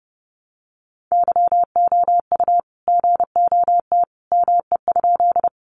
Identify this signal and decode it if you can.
Morse code, heard as a keyed tone, about 30 wpm, 700 Hz. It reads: YOU GOT ME?
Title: Sentences from Top 300 Words 30wpm